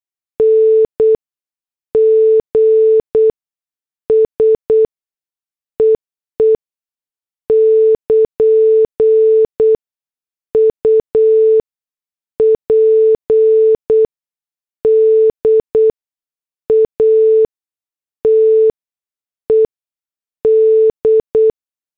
Morse code, heard as a keyed tone, about 8 wpm, spaced as for 6 wpm, 440 Hz. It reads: NGS EE <KN> UPDATED
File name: morse / headlines